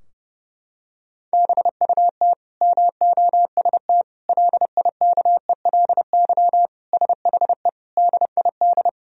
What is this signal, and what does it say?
Morse code, heard as a keyed tone, about 32 wpm, 700 Hz. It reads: BUT MOST LIKELY SHE DID
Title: Sentences from Top 1000 Words - Review Part 4 - 30wpm